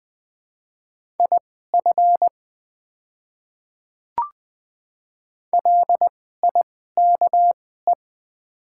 Morse code, heard as a keyed tone, about 20 wpm, spaced as for 15 wpm, 700 Hz. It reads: IF LIKE
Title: 2x Character Spacing - Top 100 Words - 20wpm